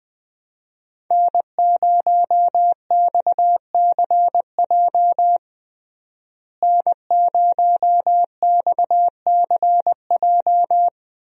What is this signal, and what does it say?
Morse code, heard as a keyed tone, about 21 wpm, 700 Hz. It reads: N0XCJ N0XCJ
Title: USA Call Signs - 5 Characters - Repeated 3x - 20wpm